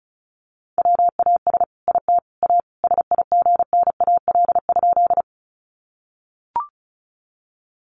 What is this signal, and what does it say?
Morse code, heard as a keyed tone, about 34 wpm, 700 Hz. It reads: WAS IT A SIGNAL?
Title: Sentences from Top 1400 Words 35wpm